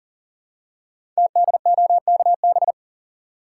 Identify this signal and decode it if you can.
Morse code, heard as a keyed tone, about 39 wpm, 700 Hz. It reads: TDOKB
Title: Code Groups - 5 Letters - 40wpm